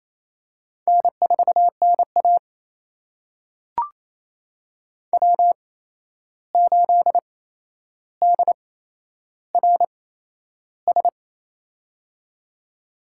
Morse code, heard as a keyed tone, about 28 wpm, spaced as for 11 wpm, 700 Hz. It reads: N4NA W8DRS
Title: USA Call Signs - 8x Character Spacing - 28wpm